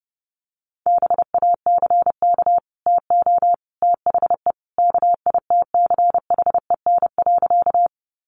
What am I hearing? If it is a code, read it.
Morse code, heard as a keyed tone, about 30 wpm, 700 Hz. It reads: BACK TO THE KITCHEN.
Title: Sentences from Top 1400 Words 30wpm